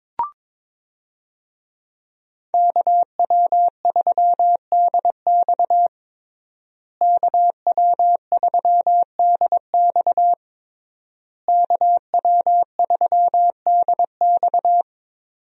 Morse code, heard as a keyed tone, about 22 wpm, 700 Hz. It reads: KW3DX KW3DX KW3DX
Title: USA Call Signs - 5 Characters - Repeated 3x - 22wpm